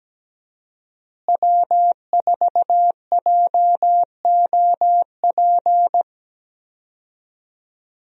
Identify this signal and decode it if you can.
Morse code, heard as a keyed tone, about 17 wpm, 700 Hz. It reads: W4JOP
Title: USA Call Signs - 5 Characters - Repeated 3x - 17wpm